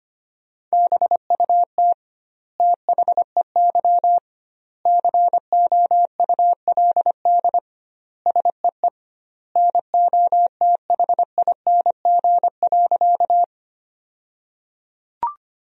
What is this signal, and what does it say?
Morse code, heard as a keyed tone, about 25 wpm, 700 Hz. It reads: BUT THEY COULD SEE NOTHING.
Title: 2x Word-Spacing - Sentences from Top 500 Words - 25wpm